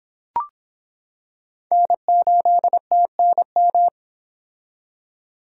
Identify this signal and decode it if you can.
Morse code, heard as a keyed tone, about 25 wpm, 700 Hz. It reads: N8TNM